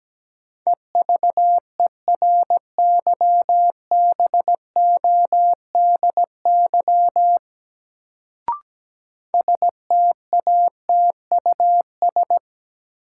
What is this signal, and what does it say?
Morse code, heard as a keyed tone, about 17 wpm, 700 Hz. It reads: EVERYBODY STATUS
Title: Sentences from Top 1400 Words - New Word Drill Alt - 17wpm